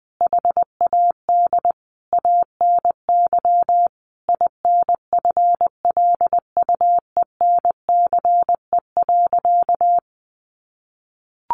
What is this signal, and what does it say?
Morse code, heard as a keyed tone, about 20 wpm, 700 Hz. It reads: HAD ANY INFLUENCE.
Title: Sentences from Top 1100 Words 20wpm